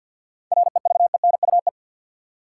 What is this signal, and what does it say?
Morse code, heard as a keyed tone, about 53 wpm, 700 Hz. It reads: REVENUE
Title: Seven Letter Words - 50wpm